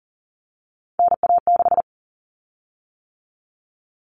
Morse code, heard as a keyed tone, about 40 wpm, 700 Hz. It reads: NA6
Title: Common Cut Numbers - 3-Digit 40wpm